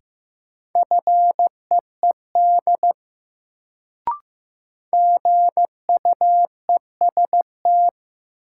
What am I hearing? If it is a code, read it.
Morse code, heard as a keyed tone, about 15 wpm, 700 Hz. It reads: FEED GUEST
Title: Sentences from Top 1600 Words - New Word Drill Alt - 15wpm